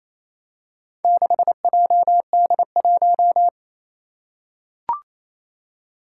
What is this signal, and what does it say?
Morse code, heard as a keyed tone, about 27 wpm, 700 Hz. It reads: 6JD1
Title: Four Challenging Letter-Number Combinations 28wpm